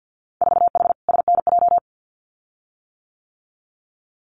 Morse code, heard as a keyed tone, about 50 wpm, 700 Hz. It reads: 4H SNJ
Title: Winter Field Day Exchanges Encore 50wpm